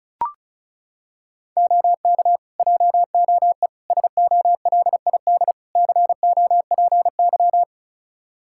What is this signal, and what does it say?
Morse code, heard as a keyed tone, about 34 wpm, 700 Hz. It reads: OK JOE SOLID COPY
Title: Practice QSO - Exchange 3 - More Info - 35wpm